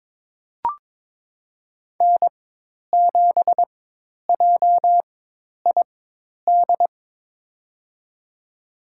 Morse code, heard as a keyed tone, about 22 wpm, spaced as for 10 wpm, 700 Hz. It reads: N7JID
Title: USA Call Signs - 4x Character Spacing - 22wpm